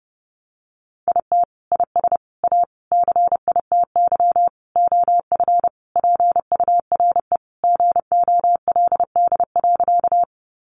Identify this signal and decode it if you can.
Morse code, heard as a keyed tone, about 30 wpm, 700 Hz. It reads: IT IS A CITY OF PURE GOLD.
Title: Sentences from Top 1500 Words 30wpm